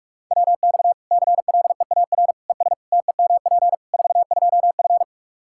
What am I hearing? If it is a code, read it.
Morse code, heard as a keyed tone, about 46 wpm, 700 Hz. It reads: WX CLEAR ES TEMP 42F
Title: Practice QSO - Exchange 2 - Info - 45wpm